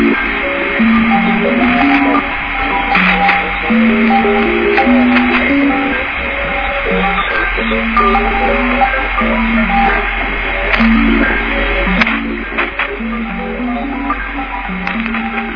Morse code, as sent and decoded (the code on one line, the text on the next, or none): none